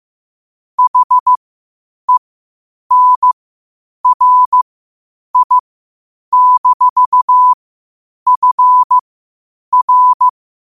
Morse code, heard as a keyed tone, about 15 wpm, spaced as for 8 wpm, 1000 Hz. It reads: HENRI-FR
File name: qotdfiles / 8wpm